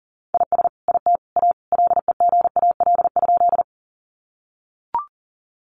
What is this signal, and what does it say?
Morse code, heard as a keyed tone, about 40 wpm, 700 Hz. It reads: IS IT A LEGAL?
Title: Sentences from Top 1200 Words 40wpm